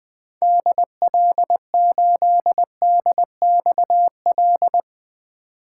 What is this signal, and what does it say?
Morse code, heard as a keyed tone, about 20 wpm, 700 Hz. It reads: DL8DXL
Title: 2023 SOTA Chaser Call Signs - 20wpm